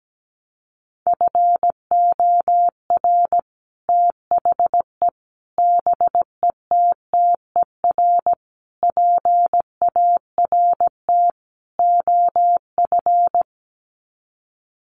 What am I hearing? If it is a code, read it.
Morse code, heard as a keyed tone, about 17 wpm, 700 Hz. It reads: FOR THE BETTER PART OF
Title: Sets of 5 Words Encore2 17wpm